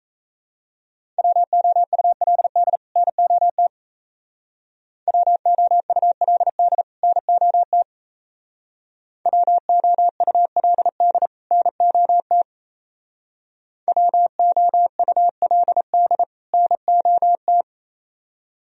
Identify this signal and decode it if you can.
Morse code, heard as a keyed tone, about 34 wpm, 700 Hz. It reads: WOULD NOT WOULD NOT WOULD NOT WOULD NOT
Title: Speed-Racing - Sets of 2 Words 28wpm